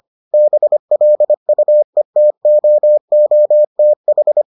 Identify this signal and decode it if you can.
Morse code, heard as a keyed tone, about 25 wpm, 600 Hz. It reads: BLUETOOTH